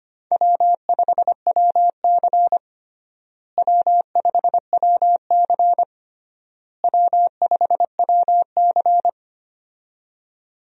Morse code, heard as a keyed tone, about 25 wpm, 700 Hz. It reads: W5WC W5WC W5WC
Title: USA Call Signs - 4 Characters - Repeated 3x - 25wpm